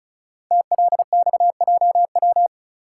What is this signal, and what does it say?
Morse code, heard as a keyed tone, about 36 wpm, 700 Hz. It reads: TLXJW
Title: Code Groups - 5 Letters - 35wpm